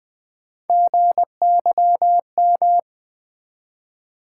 Morse code, heard as a keyed tone, about 20 wpm, 700 Hz. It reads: GYM
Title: Build A Word - 1 Letter At A Time - 3-Letter Words - 20wpm